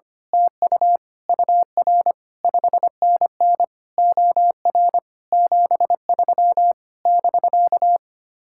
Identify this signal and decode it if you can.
Morse code, heard as a keyed tone, about 24 wpm, 700 Hz. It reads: TU UR 5NN OR 73 <BK>
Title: Speed-Racing Common QSO Phrases - 25wpm